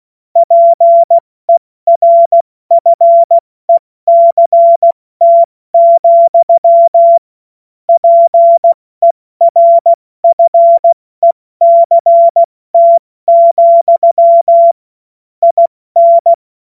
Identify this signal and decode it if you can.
Morse code, heard as a keyed tone, about 16 wpm, 680 Hz. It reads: PERFECT, PERFECT, IN
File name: Morsekurs / Witze / Witz 23 Januar 2024 70 F80